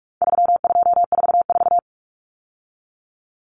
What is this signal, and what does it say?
Morse code, heard as a keyed tone, about 44 wpm, 700 Hz. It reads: 3244
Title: Common Cut Numbers - 4-Digit 45wpm